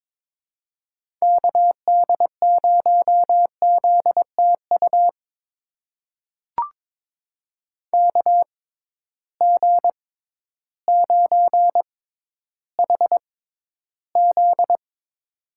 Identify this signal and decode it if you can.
Morse code, heard as a keyed tone, about 22 wpm, spaced as for 12 wpm, 700 Hz. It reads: KD0ZTU KG9HZ
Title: USA Call Signs - 6x Character Spacing - 22wpm